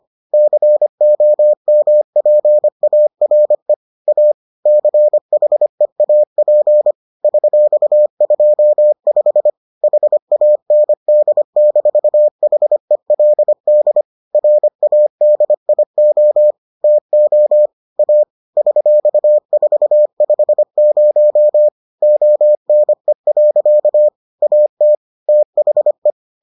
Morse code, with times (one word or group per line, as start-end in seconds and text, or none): 0.3-3.7 COMPARE
4.1-4.3 A
4.7-6.9 CHEAP
7.2-9.5 $25
9.8-14.0 HAND-HELD
14.3-16.5 RADIO
16.8-17.7 TO
18.0-18.2 A
18.6-21.7 $450
22.0-24.1 ONE.
24.4-24.9 AT
25.3-26.1 THE